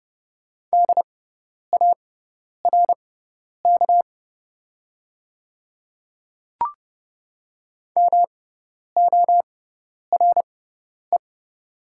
Morse code, heard as a keyed tone, about 30 wpm, spaced as for 10 wpm, 700 Hz. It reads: DARK MORE
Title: 6x Character Spacing - Top 500 Words - 30wpm